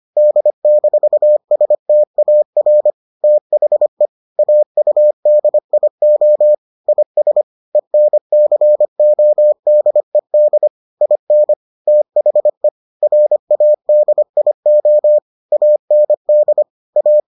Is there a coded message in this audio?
Yes: Morse code from 0.2 to 17.2 s, heard as a keyed tone, about 25 wpm, 600 Hz. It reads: D-STAR THE AUDIO IS ENCODED IN THE RADIO AND A